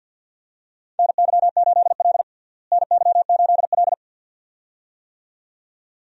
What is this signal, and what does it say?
Morse code, heard as a keyed tone, about 51 wpm, 700 Hz. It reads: NY8L NY8L
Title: USA Call Signs - 4 Characters - Repeated 2x - 50wpm